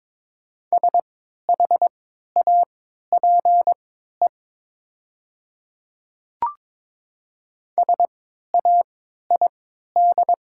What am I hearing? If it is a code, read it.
Morse code, heard as a keyed tone, about 22 wpm, spaced as for 12 wpm, 700 Hz. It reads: SHAPE SAID
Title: 3x Character Spacing - Top 500 Words - 22wpm